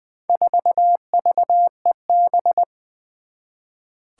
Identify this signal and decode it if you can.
Morse code, heard as a keyed tone, about 20 wpm, 700 Hz. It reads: 4VEB
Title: Four Challenging Letter-Number Combinations 20wpm